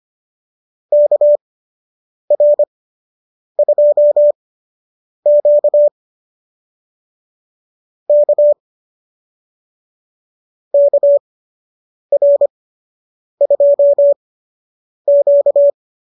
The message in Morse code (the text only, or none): KR2Q K KR2Q